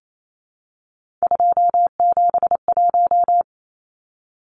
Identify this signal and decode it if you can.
Morse code, heard as a keyed tone, about 28 wpm, 700 Hz. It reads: 271
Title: Numbers - 3-Digit 28wpm